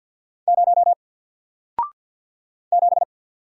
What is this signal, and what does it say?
Morse code, heard as a keyed tone, about 48 wpm, 700 Hz. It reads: 0 7
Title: Single Letter-Number Alt 50wpm